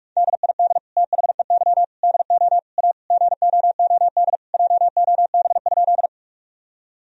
Morse code, heard as a keyed tone, about 46 wpm, 700 Hz. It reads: DID THEY DO A GOOD JOB?
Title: Sentences from Top 1000 Words - Review Part 3 - 45wpm